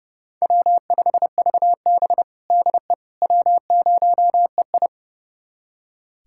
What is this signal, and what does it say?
Morse code, heard as a keyed tone, about 30 wpm, 700 Hz. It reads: W5VB DE W0EI
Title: USA Callsign DE Callsign - 4 Characters - 30wpm